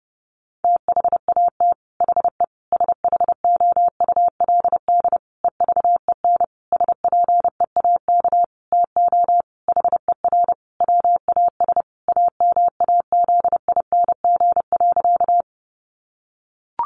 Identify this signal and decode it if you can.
Morse code, heard as a keyed tone, about 28 wpm, 700 Hz. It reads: THAT HE SHOULD EVEN SPEAK TO HER WAS AMAZING.
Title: Sentences from Top 1500 Words 30wpm